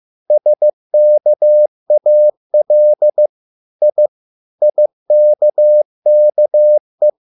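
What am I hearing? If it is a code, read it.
Morse code, heard as a keyed tone, about 15 wpm, 600 Hz. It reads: SKAL I IKKE
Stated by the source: Morse code at a slow 15 wpm from CW med Gnister.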